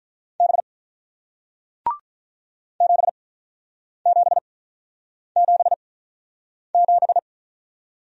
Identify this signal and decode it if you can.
Morse code, heard as a keyed tone, about 46 wpm, spaced as for 9 wpm, 700 Hz. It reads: B 7777